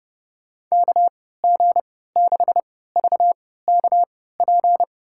Morse code, heard as a keyed tone, about 32 wpm, spaced as for 17 wpm, 700 Hz. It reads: KG6VKP
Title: USA Call Signs - 3x Character Spacing - 30wpm